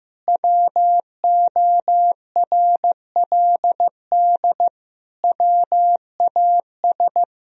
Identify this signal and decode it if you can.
Morse code, heard as a keyed tone, about 15 wpm, 700 Hz. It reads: WORLD WAS
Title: Sentences from Top 1000 Words - Review Part 1 - 15wpm